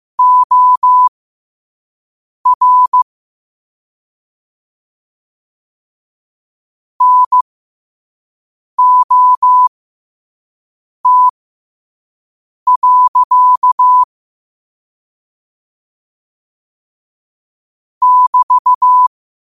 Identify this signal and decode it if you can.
Morse code, heard as a keyed tone, about 15 wpm, spaced as for 5 wpm, 1000 Hz. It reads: OR NOT. <BT>
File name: qotdfiles / 5wpm